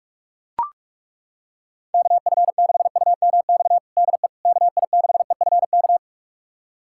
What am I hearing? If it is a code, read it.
Morse code, heard as a keyed tone, about 41 wpm, 700 Hz. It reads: KF6UMX DE KI6EFK
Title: USA Callsign DE Callsign - 6 Characters - 45wpm